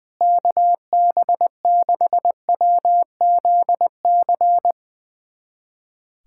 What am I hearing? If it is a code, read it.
Morse code, heard as a keyed tone, about 20 wpm, 700 Hz. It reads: KB6WZC
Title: USA Callsign DE Callsign - 6 Characters - 20wpm